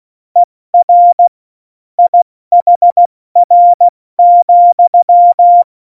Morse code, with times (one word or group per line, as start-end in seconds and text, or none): 0.4-1.3 ER
2.0-5.6 IHR,